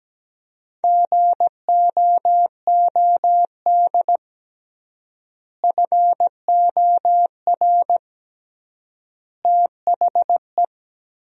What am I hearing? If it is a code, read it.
Morse code, heard as a keyed tone, about 17 wpm, 700 Hz. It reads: GOOD FOR THE
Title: 3x Word-Spacing - Sentences from Top 300 Words - 17wpm